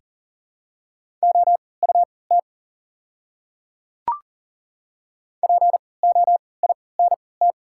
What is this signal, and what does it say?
Morse code, heard as a keyed tone, about 40 wpm, spaced as for 23 wpm, 690 Hz. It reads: OUT POINT
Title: 3x Character Spacing - Top 200 Words - 40wpm